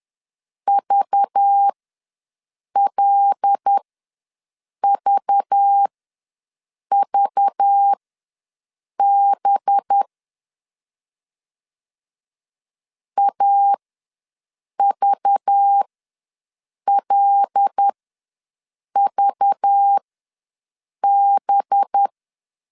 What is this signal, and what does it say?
Morse code, heard as a keyed tone, about 10 wpm, spaced as for 6 wpm, 800 Hz. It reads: VLVVB AVLVB
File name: LEZIONI / 5 LBVC / 5 LBV ws 10wpm 3645